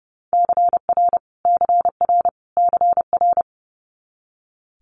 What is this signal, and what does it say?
Morse code, heard as a keyed tone, about 30 wpm, 700 Hz. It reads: CR CR CR
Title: Multi-letter Phonemes Repeated 3x 30wpm